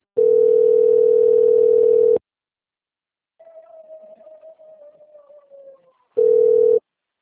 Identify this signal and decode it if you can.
Ringback tone (North America), 440 + 480 Hz.
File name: out- - - - - .42443